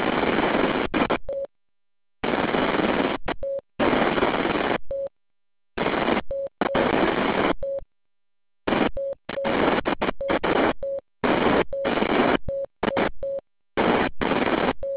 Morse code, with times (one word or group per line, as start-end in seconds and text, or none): none